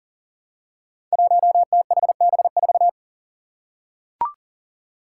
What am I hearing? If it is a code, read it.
Morse code, heard as a keyed tone, about 40 wpm, 700 Hz. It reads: 1THB4
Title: Five Challenging Letter-Number Combinations 40wpm